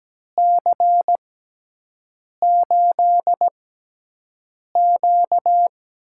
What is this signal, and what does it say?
Morse code, heard as a keyed tone, about 17 wpm, spaced as for 6 wpm, 700 Hz. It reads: C8Q
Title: USA Call Signs - 6x Character Spacing - 17wpm